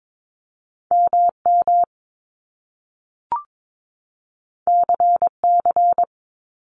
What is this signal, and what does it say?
Morse code, heard as a keyed tone, about 22 wpm, 700 Hz. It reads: MM CC
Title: Top Two Letter Combinations in Words 22wpm